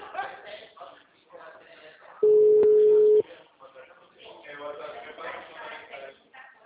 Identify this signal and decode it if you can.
Ringback tone (Europe, 425 Hz), with speech.